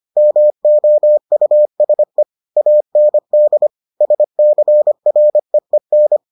Morse code, heard as a keyed tone, about 25 wpm, 600 Hz. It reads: MOUSE AND SCREEN